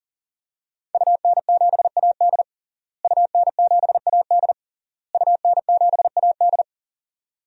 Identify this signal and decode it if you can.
Morse code, heard as a keyed tone, about 37 wpm, 700 Hz. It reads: UN7AD UN7AD UN7AD
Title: DXCC-100 Call Signs - 5 Characters - Repeated 3x - 40wpm